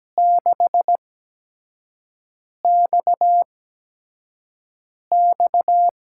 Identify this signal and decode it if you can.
Morse code, heard as a keyed tone, about 17 wpm, spaced as for 5 wpm, 700 Hz. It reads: 6XX
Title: USA Call Signs - 5 Characters - 8x Character Spacing - 17wpm